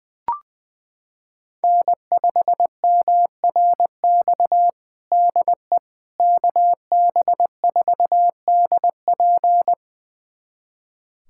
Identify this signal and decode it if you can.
Morse code, heard as a keyed tone, about 20 wpm, 700 Hz. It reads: N5MRX DE KB4DP
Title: USA Callsign DE Callsign - 5 Characters - 20wpm